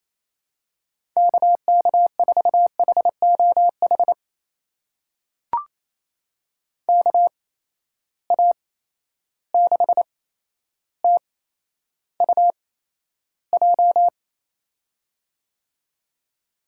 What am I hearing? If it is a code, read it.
Morse code, heard as a keyed tone, about 28 wpm, spaced as for 12 wpm, 700 Hz. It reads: KK4HOH KA6TUJ
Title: USA Call Signs - 6 Characters - 8x Character Spacing - 28wpm